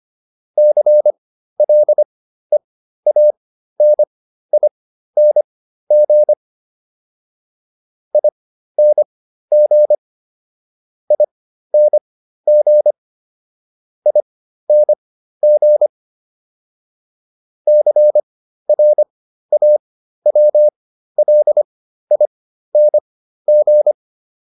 Morse code, heard as a keyed tone, about 25 wpm, spaced as for 13 wpm, 600 Hz. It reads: CLEANING ING ING ING CRAWLING
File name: ING8 13